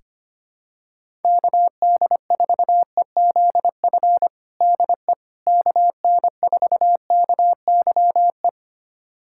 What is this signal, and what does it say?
Morse code, heard as a keyed tone, about 25 wpm, 700 Hz. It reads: KD4EZF DE KN4KYE
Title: USA Callsign DE Callsign 25wpm